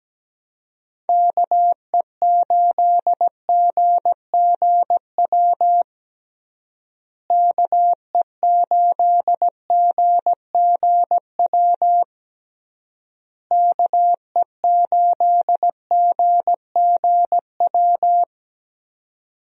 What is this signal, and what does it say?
Morse code, heard as a keyed tone, about 17 wpm, 700 Hz. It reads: KE8GGW KE8GGW KE8GGW
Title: USA Call Signs - 6 Characters - Repeated 3x - 17wpm